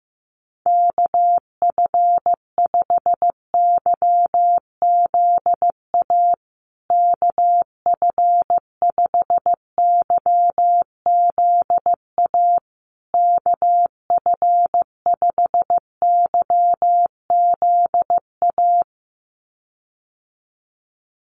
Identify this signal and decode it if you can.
Morse code, heard as a keyed tone, about 15 wpm, 700 Hz. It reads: KF5YZA KF5YZA KF5YZA
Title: USA Call Signs Repeated 3x 15wpm